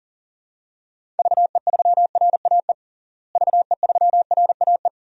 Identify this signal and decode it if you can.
Morse code, heard as a keyed tone, about 40 wpm, 700 Hz. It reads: VE3RAE VE3RAE
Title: DXCC-100 Call Signs - 6 Characters - Repeated 2x - 40wpm